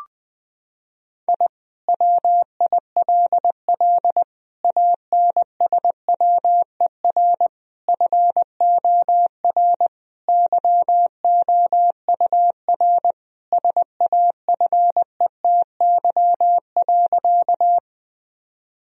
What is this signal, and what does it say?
Morse code, heard as a keyed tone, about 20 wpm, 700 Hz. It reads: I WILL ANSWER FOR YOUR SAFETY.
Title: Sentences from Top 1000 Words - Review Part 4 - 20wpm